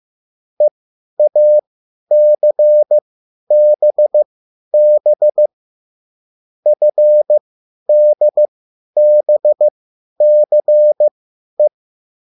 Morse code, heard as a keyed tone, about 15 wpm, spaced as for 10 wpm, 600 Hz. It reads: EACBB FDBCE
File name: QSL / CW lesson